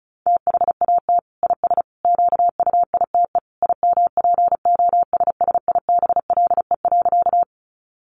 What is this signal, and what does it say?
Morse code, heard as a keyed tone, about 34 wpm, 700 Hz. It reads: THAT IS QUITE IMPOSSIBLE.
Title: Sentences from Top 1100 Words 35wpm